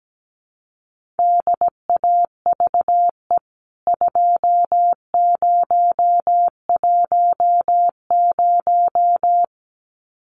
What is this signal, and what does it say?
Morse code, heard as a keyed tone, about 17 wpm, 700 Hz. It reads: DAVE 2010
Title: CWT Exchange 17wpm